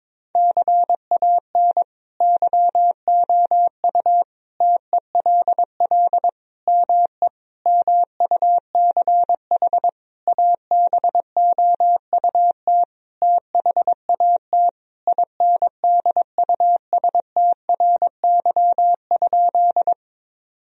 Morse code, heard as a keyed tone, about 22 wpm, 700 Hz. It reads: CAN YOU TELL ME MUCH ABOUT THAT INDUSTRY?
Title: Sentences from Top 1000 Words - Review Part 4 - 22wpm